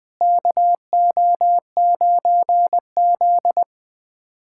Morse code, heard as a keyed tone, about 20 wpm, 700 Hz. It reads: KO9Z
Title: USA Callsign DE Callsign 20wpm